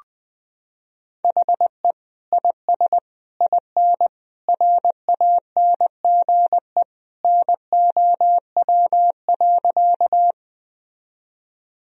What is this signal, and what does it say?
Morse code, heard as a keyed tone, about 20 wpm, 700 Hz. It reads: HE IS IN RANGE NOW.
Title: Sentences from Top 1000 Words - Review Part 4 - 20wpm